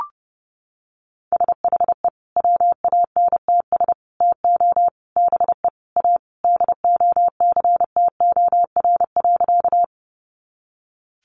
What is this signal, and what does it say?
Morse code, heard as a keyed tone, about 28 wpm, 700 Hz. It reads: SHE WANTS TO BE A DOCTOR.
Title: Easy Phrases 30wpm